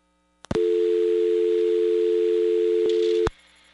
Dial tone (North America), 350 + 440 Hz.